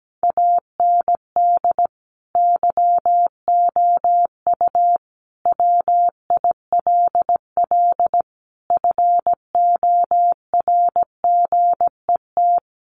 Morse code, heard as a keyed tone, about 17 wpm, 700 Hz. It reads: AND YOU WILL FORGET
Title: Sentences from Top 1200 Words 17wpm